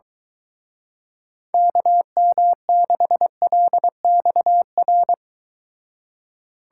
Morse code, heard as a keyed tone, about 22 wpm, 700 Hz. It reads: KM6LXR